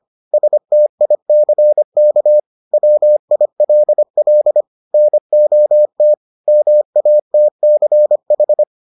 Morse code, heard as a keyed tone, about 25 wpm, 600 Hz. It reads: STICK WILL NOT MATCH